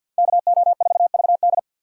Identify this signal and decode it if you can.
Morse code, heard as a keyed tone, about 65 wpm, 700 Hz. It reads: KO4VD